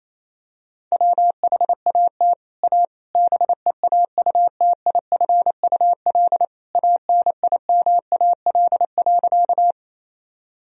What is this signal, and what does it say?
Morse code, heard as a keyed tone, about 28 wpm, 700 Hz. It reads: WHAT A BEAUTIFUL ANIMAL.
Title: Sentences from Top 1000 Words - Review Part 1 - 28wpm